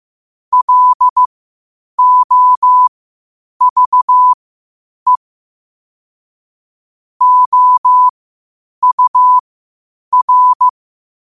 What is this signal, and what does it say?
Morse code, heard as a keyed tone, about 15 wpm, spaced as for 8 wpm, 1000 Hz. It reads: LOVE OUR
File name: qotdfiles / 8wpm